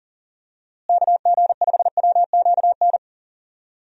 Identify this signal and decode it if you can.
Morse code, heard as a keyed tone, about 40 wpm, 700 Hz. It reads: KG5WQN
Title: USA Call Signs - 6 Characters - Repeated 3x - 40wpm